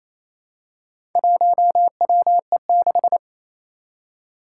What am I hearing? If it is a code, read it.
Morse code, heard as a keyed tone, about 27 wpm, 700 Hz. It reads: 1WE6
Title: Four Challenging Letter-Number Combinations 28wpm